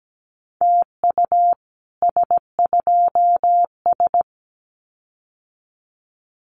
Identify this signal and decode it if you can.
Morse code, heard as a keyed tone, about 17 wpm, 700 Hz. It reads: TU S2S
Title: Common QSO Phrases Encore 17wpm